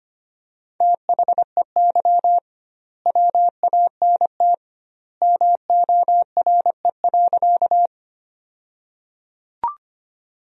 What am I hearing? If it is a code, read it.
Morse code, heard as a keyed tone, about 25 wpm, 700 Hz. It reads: THEY WANT MORE.
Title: 2x Word-Spacing - Sentences from Top 200 Words - 25wpm